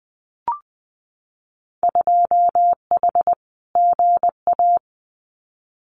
Morse code, heard as a keyed tone, about 20 wpm, 700 Hz. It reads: 2H GA
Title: Winter Field Day Exchanges Encore 20wpm